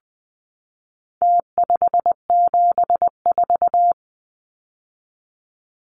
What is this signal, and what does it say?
Morse code, heard as a keyed tone, about 20 wpm, 700 Hz. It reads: T574
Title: Numbers - Contest Style - 4-Digit 20wpm